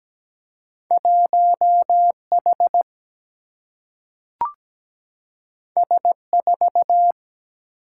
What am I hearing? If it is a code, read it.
Morse code, heard as a keyed tone, about 17 wpm, 700 Hz. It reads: 1H S4
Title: Single-Letter Single-Number 17wpm